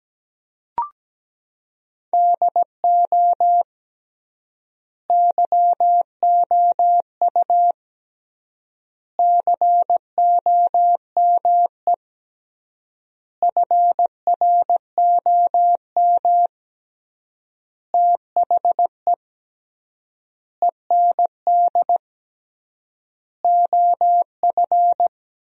Morse code, heard as a keyed tone, about 17 wpm, 700 Hz. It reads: DO YOU COME FROM THE END OF
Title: 3x Word-Spacing - Sentences from Top 200 Words - 17wpm